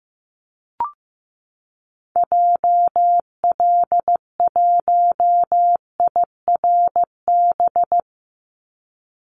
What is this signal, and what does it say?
Morse code, heard as a keyed tone, about 15 wpm, 700 Hz. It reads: JL1IRB
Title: Call Signs Repeated 3x - 15wpm